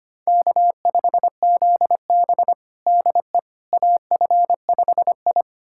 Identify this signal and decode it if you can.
Morse code, heard as a keyed tone, about 25 wpm, 700 Hz. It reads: K5ZB DE AF5I
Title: USA Callsign DE Callsign - 4 Characters - 25wpm